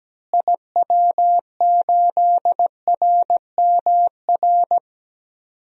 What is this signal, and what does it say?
Morse code, heard as a keyed tone, about 17 wpm, 700 Hz. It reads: IW8RMR
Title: DXCC-100 Call Signs - 6 Characters - Repeated 3x - 17wpm